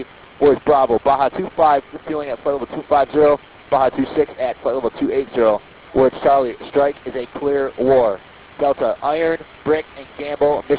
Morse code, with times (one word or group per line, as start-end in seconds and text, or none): none